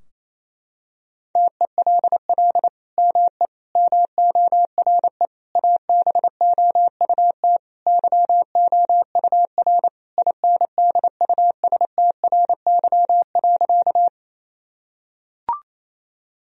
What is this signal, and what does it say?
Morse code, heard as a keyed tone, about 28 wpm, 700 Hz. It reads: TELL ME MORE ABOUT YOUR INDUSTRY.
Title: Sentences from Top 1000 Words - Review Part 4 - 28wpm